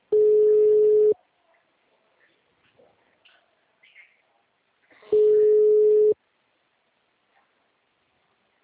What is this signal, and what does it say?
Ringback tone (Europe, 425 Hz).
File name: out- -2025- - - .41018